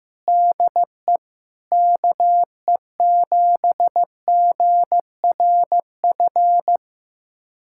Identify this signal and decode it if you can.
Morse code, heard as a keyed tone, about 15 wpm, 700 Hz. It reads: DE KE7GRF
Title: USA Callsign DE Callsign 15wpm